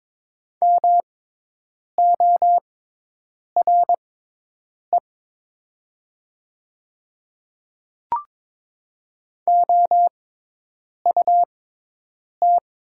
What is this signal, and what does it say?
Morse code, heard as a keyed tone, about 22 wpm, spaced as for 8 wpm, 700 Hz. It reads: MORE OUT
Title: 6x Character Spacing - Top 100 Words - 22wpm